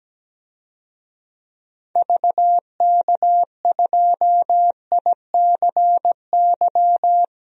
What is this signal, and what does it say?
Morse code, heard as a keyed tone, about 17 wpm, 700 Hz. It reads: VK2ICY
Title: DXCC-100 Call Signs - 6 Characters - 17wpm